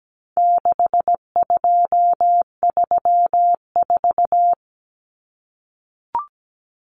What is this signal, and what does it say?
Morse code, heard as a keyed tone, about 17 wpm, 700 Hz. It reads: 6234
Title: Common Cut Numbers - 4-Digit 17wpm